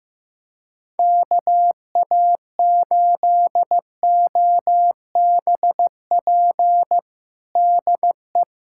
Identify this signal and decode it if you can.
Morse code, heard as a keyed tone, about 16 wpm, 700 Hz. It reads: KA8OBP DE
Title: USA Callsign DE Callsign - 6 Characters - 15wpm